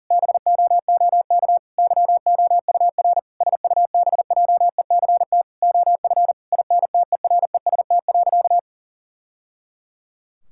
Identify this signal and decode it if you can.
Morse code, heard as a keyed tone, about 37 wpm, 700 Hz. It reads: BOOK YOUR SUBJECT OF INTEREST.
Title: Sentences from Top 1000 Words - Review Part 5 - 40wpm